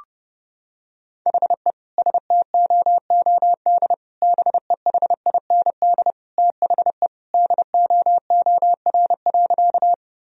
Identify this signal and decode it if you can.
Morse code, heard as a keyed tone, about 32 wpm, 700 Hz. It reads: HE STOOD BEHIND THE DOOR.
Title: Sentences from Top 1000 Words - Review Part 2 - 30wpm